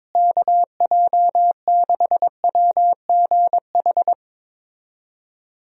Morse code, heard as a keyed tone, about 22 wpm, 700 Hz. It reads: KJ6WGH
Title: USA Call Signs - 6 Characters - Repeated 2x - 22wpm